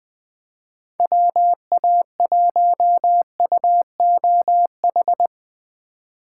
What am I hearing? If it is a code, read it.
Morse code, heard as a keyed tone, about 21 wpm, 700 Hz. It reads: WA1UOH